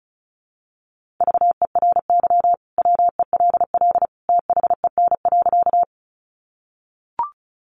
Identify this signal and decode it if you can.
Morse code, heard as a keyed tone, about 34 wpm, 700 Hz. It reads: VERY WELL THEN.
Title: Sentences from Top 200 Words 35wpm